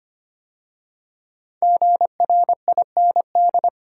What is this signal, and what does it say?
Morse code, heard as a keyed tone, about 25 wpm, 700 Hz. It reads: GRIND
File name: 5 Letter Words - First 3 Letters Given - 25wpm